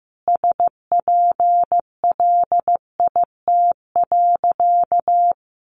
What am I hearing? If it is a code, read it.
Morse code, heard as a keyed tone, about 15 wpm, 700 Hz. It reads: SPLIT.
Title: Additional Easy Phrases 15wpm